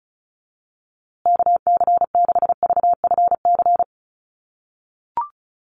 Morse code, heard as a keyed tone, about 34 wpm, 700 Hz. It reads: KC6VFC
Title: USA Call Signs Repeated 2x 35wpm